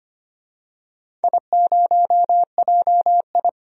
Morse code, heard as a keyed tone, about 25 wpm, 700 Hz. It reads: I0JI
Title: DXCC-100 Call Signs - 4 Characters - 25wpm